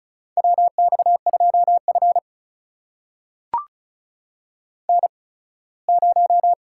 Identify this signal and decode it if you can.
Morse code, heard as a keyed tone, about 36 wpm, spaced as for 21 wpm, 700 Hz. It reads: WX2F N0
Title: USA Call Signs - 4 Characters - 8x Character Spacing - 35wpm